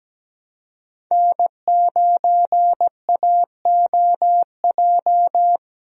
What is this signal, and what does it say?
Morse code, heard as a keyed tone, about 17 wpm, 700 Hz. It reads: N9AOJ
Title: USA Call Signs - 5 Characters - Repeated 3x - 17wpm